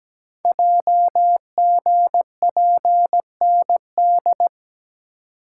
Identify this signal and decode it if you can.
Morse code, heard as a keyed tone, about 17 wpm, 700 Hz. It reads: JGPND
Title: Code Groups - 5 Letters - 17wpm